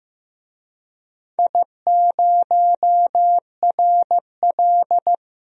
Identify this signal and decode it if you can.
Morse code, heard as a keyed tone, about 15 wpm, 700 Hz. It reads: I0RL
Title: DXCC-100 Call Signs - 4 Characters - 15wpm